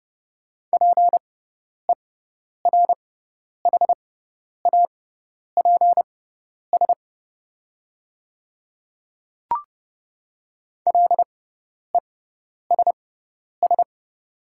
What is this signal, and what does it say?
Morse code, heard as a keyed tone, about 30 wpm, spaced as for 10 wpm, 700 Hz. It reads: PERHAPS LESS
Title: 6x Character Spacing - Top 500 Words - 30wpm